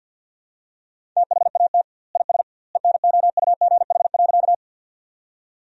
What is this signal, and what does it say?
Morse code, heard as a keyed tone, about 52 wpm, 700 Hz. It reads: THAT IS ENOUGH.
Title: 2x Word-Spacing - Sentences from Top 400 Words - 50wpm